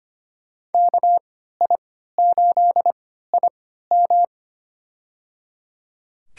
Morse code, heard as a keyed tone, about 25 wpm, spaced as for 14 wpm, 700 Hz. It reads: KI8IM
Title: USA Call Signs - 3x Character Spacing - 25wpm